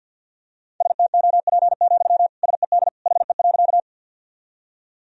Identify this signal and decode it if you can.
Morse code, heard as a keyed tone, about 52 wpm, 700 Hz. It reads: STOP, SED HE.
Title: Sentences from Top 300 Words and Abbreviations 50wpm